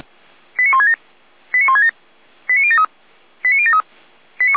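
Five-tone selective call (ZVEI), heard as quick runs of five tones, tones of about 80 ms. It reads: 89167 89167 89073 89073
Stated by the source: ZVEI five-tone selective call received via SDR.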